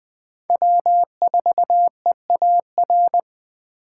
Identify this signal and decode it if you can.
Morse code, heard as a keyed tone, about 20 wpm, 700 Hz. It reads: W4EAR
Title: Speed Racing - USA 1x3 Call Signs 17wpm